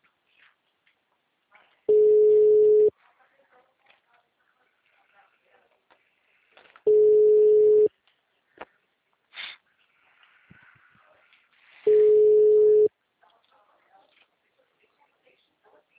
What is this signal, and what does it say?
Ringback tone (Europe, 425 Hz), with speech.